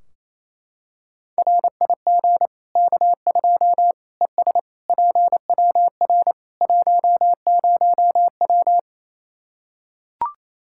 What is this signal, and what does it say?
Morse code, heard as a keyed tone, about 27 wpm, 700 Hz. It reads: RIG K2 ES PWR 10W
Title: Practice QSO - Exchange 2 - Info - 28wpm